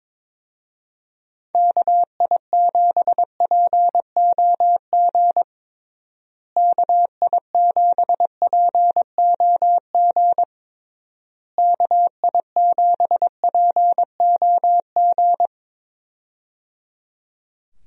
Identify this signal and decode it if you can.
Morse code, heard as a keyed tone, about 22 wpm, 700 Hz. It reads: KI7POG KI7POG KI7POG
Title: USA Call Signs - 6 Characters - Repeated 3x - 22wpm